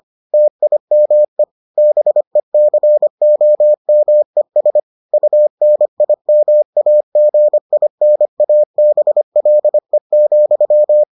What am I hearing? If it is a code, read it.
Morse code, heard as a keyed tone, about 25 wpm, 600 Hz. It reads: TIME BECOMES UNIMAGINABLE,